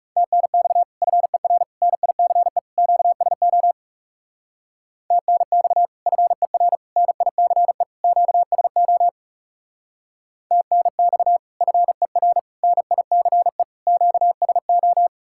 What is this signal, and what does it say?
Morse code, heard as a keyed tone, about 39 wpm, 700 Hz. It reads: TNX FER NICE QSO TNX FER NICE QSO TNX FER NICE QSO